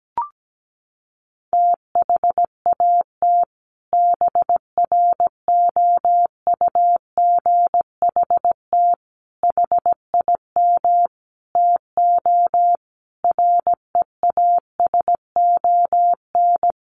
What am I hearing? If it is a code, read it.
Morse code, heard as a keyed tone, about 17 wpm, 700 Hz. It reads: THAT BROUGHT HIM TO REASON
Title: Sentences from Top 700 Words 17wpm